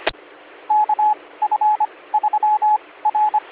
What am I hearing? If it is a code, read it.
Morse code, heard as a keyed tone, about 24 wpm, 850 Hz, with radio noise. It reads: K F 3 R